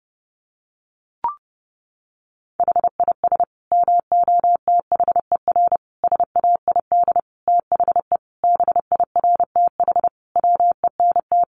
Morse code, heard as a keyed tone, about 30 wpm, 700 Hz. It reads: HIS MOTHER SAID THE BIRTH WENT
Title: Sentences from Top 800 Words 30wpm